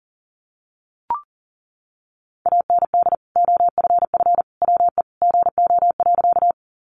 Morse code, heard as a keyed tone, about 40 wpm, 700 Hz. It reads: AND OFF WE GO.
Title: Sentences from Top 200 Words 40wpm